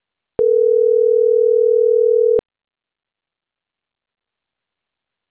Ringback tone (North America), 440 + 480 Hz.